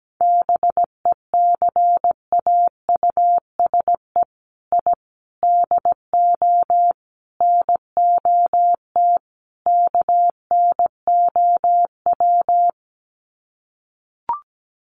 Morse code, heard as a keyed tone, about 17 wpm, 700 Hz. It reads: BECAUSE I DO NOT KNOW
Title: Sets of 5 Words Encore2 17wpm